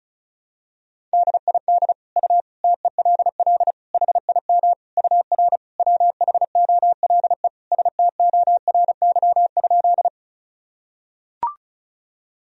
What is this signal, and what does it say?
Morse code, heard as a keyed tone, about 34 wpm, 700 Hz. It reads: DID U TELL HIM UR WHOLE STORY?
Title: Sentences from Top 400 Words and Abbreviations 35wpm